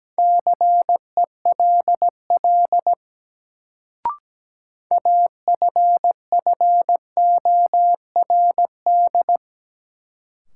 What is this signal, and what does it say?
Morse code, heard as a keyed tone, about 17 wpm, 700 Hz. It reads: CELL AFFORD
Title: Sentences from Top 1100 Words - New Word Drill - 17wpm